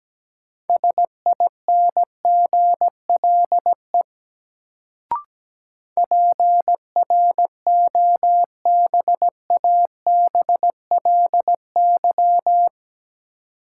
Morse code, heard as a keyed tone, about 17 wpm, 700 Hz. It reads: SINGLE PROBABLY
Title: Sentences from Top 700 Words - New Word Drill - 17wpm